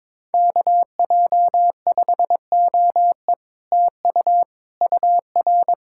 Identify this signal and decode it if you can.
Morse code, heard as a keyed tone, about 22 wpm, 700 Hz. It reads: KJ5OE TU UR
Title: Speed-Racing Common QSO Phrases - 22wpm